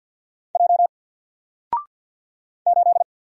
Morse code, heard as a keyed tone, about 50 wpm, 700 Hz. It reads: J 8
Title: Single Letter-Number 50wpm